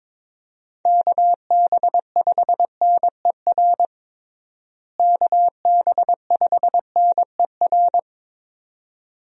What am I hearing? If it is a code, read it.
Morse code, heard as a keyed tone, about 22 wpm, 700 Hz. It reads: KB5NER KB5NER
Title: USA Call Signs - 6 Characters - Repeated 2x - 22wpm